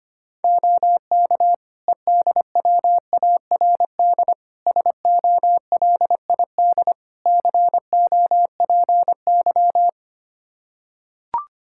Morse code, heard as a keyed tone, about 25 wpm, 700 Hz. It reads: OK EDWARD SOLID COPY
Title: Practice QSO - Exchange 3 - More Info - 25wpm